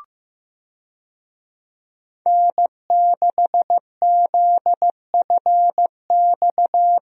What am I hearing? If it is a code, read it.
Morse code, heard as a keyed tone, about 15 wpm, 700 Hz. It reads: N6ZFX
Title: USA Call Signs - 5 Characters - Repeated 3x - 15wpm